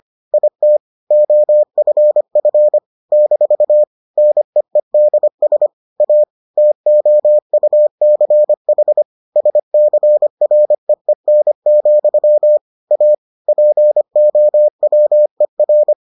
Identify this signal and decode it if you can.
Morse code, heard as a keyed tone, about 25 wpm, 600 Hz. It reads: IT OFF - NEEDS A TOUCH SCREEN, A POWER